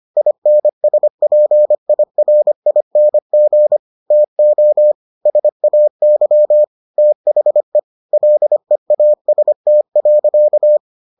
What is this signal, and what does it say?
Morse code, heard as a keyed tone, about 25 wpm, 600 Hz. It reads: INSPIRING TO SAY THE LEAST.